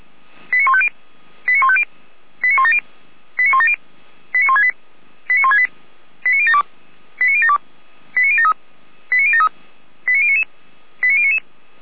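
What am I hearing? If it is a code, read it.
Five-tone selective call (ZVEI), heard as quick runs of five tones, tones of about 70 ms. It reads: 89150 89150 89170 89170 89168 89168 89072 89072 89073 89073 89099 89099
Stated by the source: ZVEI five-tone call received by SDR.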